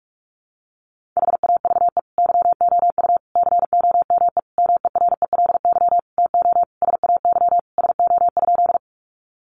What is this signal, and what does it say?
Morse code, heard as a keyed tone, about 44 wpm, 700 Hz. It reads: HAVE YOU COME MERELY TO SAY SO?
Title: Sentences from Top 1400 Words 45wpm